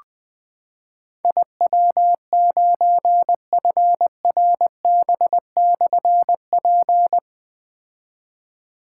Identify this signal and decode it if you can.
Morse code, heard as a keyed tone, about 20 wpm, 700 Hz. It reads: IW9FRB/P
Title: Difficult Call Signs 20wpm